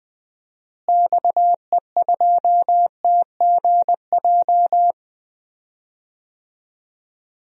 Morse code, heard as a keyed tone, about 20 wpm, 700 Hz. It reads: XE2TGJ
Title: DXCC-100 Call Signs - 6 Characters - Repeated 2x - 20wpm